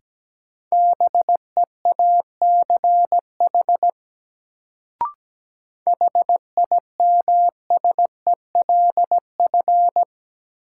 Morse code, heard as a keyed tone, about 17 wpm, 700 Hz. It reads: BEACH HIMSELF
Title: Sentences from Top 1500 Words - New Word Drill Alt - 17wpm